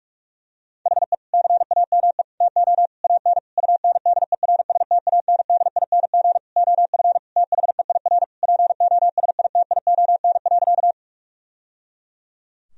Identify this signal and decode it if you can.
Morse code, heard as a keyed tone, about 46 wpm, 700 Hz. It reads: HE CAME TO AN UNDERSTANDING OF THEIR POSITION.
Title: Sentences from Top 1000 Words - Review Part 3 - 45wpm